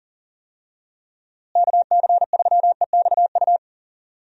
Morse code, heard as a keyed tone, about 40 wpm, 700 Hz. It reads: KC3EXU
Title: USA Call Signs - 6 Characters - Repeated 2x - 40wpm